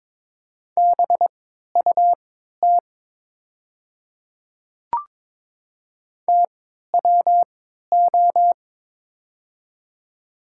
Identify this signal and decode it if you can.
Morse code, heard as a keyed tone, about 23 wpm, spaced as for 13 wpm, 700 Hz. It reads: BUT TWO
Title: 3x Character Spacing - Top 200 Words - 22wpm